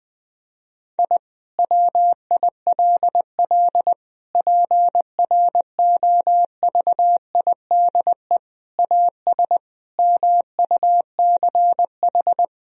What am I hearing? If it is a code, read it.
Morse code, heard as a keyed tone, about 21 wpm, 700 Hz. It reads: I WILL PROVIDE AS MUCH
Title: Sentences from Top 1000 Words - Review Part 3 - 20wpm